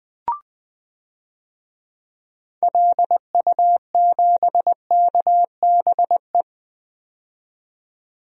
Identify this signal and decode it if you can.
Morse code, heard as a keyed tone, about 20 wpm, 700 Hz. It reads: LU7KBE
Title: DXCC-100 Call Signs - 6 Characters - 20wpm